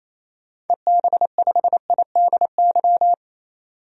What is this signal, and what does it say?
Morse code, heard as a keyed tone, about 28 wpm, 700 Hz. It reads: EB5IDY
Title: DXCC-100 Call Signs - 6 Characters - 28wpm